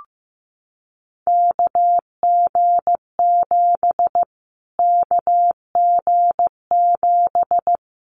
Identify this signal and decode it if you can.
Morse code, heard as a keyed tone, about 15 wpm, 700 Hz. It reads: KG7 KG7
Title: Callsign Prefixes for Top 25 DXCC - Repeated 3x - 15wpm